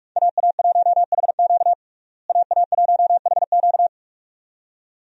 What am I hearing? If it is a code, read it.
Morse code, heard as a keyed tone, about 46 wpm, 700 Hz. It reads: AA1HQ AA1HQ
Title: USA Call Signs - 5 Characters - Repeated 2x - 45wpm